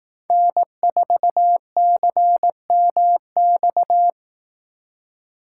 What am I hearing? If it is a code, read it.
Morse code, heard as a keyed tone, about 19 wpm, 700 Hz. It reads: N4CMX